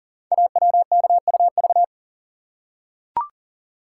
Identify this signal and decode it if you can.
Morse code, heard as a keyed tone, about 43 wpm, 690 Hz. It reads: AWKUV
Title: Code Groups - 5 Letters - 40wpm